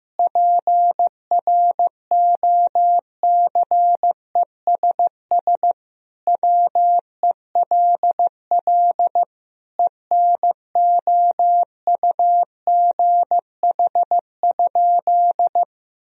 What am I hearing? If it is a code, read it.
Morse code, heard as a keyed tone, about 16 wpm, 700 Hz. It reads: PROCESS WELL ENOUGH?
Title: Sentences from Top 1000 Words - Review Part 3 - 15wpm